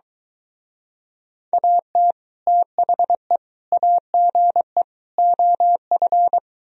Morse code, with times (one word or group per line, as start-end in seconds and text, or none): 1.5-2.1 AT
2.5-3.4 THE
3.7-4.8 AGE
5.2-6.4 OF